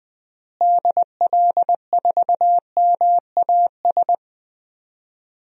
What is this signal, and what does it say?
Morse code, heard as a keyed tone, about 20 wpm, 700 Hz. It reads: DL4MAS
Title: DXCC-100 Call Signs - 6 Characters - Repeated 3x - 20wpm